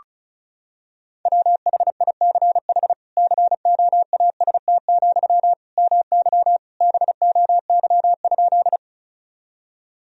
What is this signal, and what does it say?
Morse code, heard as a keyed tone, about 34 wpm, 700 Hz. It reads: WHICH COAST, MY BOY?
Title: Sentences from Top 1000 Words - Review Part 4 - 35wpm